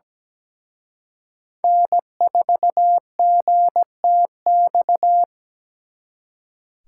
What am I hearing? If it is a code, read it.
Morse code, heard as a keyed tone, about 20 wpm, 700 Hz. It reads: N4GTX
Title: Speed Racing - USA 1x3 Call Signs 17wpm